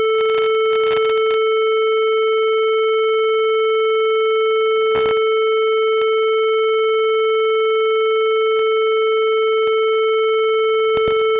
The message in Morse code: none